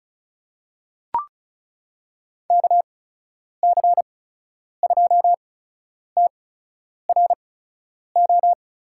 Morse code, heard as a keyed tone, about 34 wpm, spaced as for 10 wpm, 700 Hz. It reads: KC2TRO
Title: USA Call Signs - 6 Characters - 8x Character Spacing - 35wpm